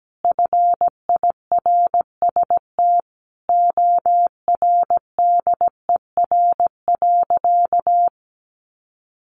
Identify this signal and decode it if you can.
Morse code, heard as a keyed tone, about 17 wpm, 700 Hz. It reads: FIRST ORDER.
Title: Sentences from Top 1100 Words 17wpm